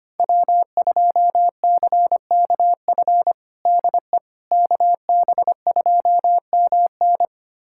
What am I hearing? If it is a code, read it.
Morse code, heard as a keyed tone, about 25 wpm, 700 Hz. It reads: W2CKF DE KB2MN
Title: USA Callsign DE Callsign - 5 Characters - 25wpm